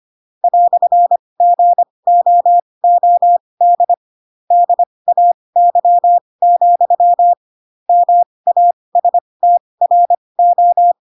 Morse code, heard as a keyed tone, about 25 wpm, spaced as for 20 wpm, 700 Hz. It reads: "GOOD DAY, MASTRO
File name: Pinocchio / Pinocchio ch2 15 25 ik1hge